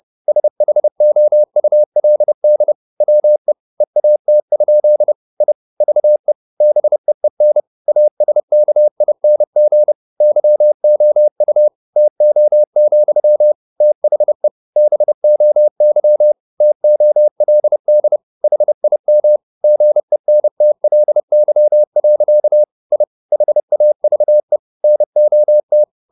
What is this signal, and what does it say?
Morse code, heard as a keyed tone, about 30 wpm, 600 Hz. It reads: SHOULD WE EAT? I VE BEEN ASKING YOU TO, THE BOY TOLD HIM GENTLY. I HAVE NOT
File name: the old man and the sea / the old man and the sea 1 30wpm